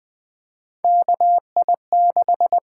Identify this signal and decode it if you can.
Morse code, heard as a keyed tone, about 20 wpm, 700 Hz. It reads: KI6
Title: USA Call Signs - 6 Characters - 8x Character Spacing - 20wpm